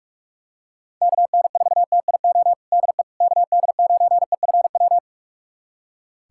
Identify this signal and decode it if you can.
Morse code, heard as a keyed tone, about 46 wpm, 700 Hz. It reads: KN4TIO DE KD9EFW
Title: USA Callsign DE Callsign - 6 Characters - 45wpm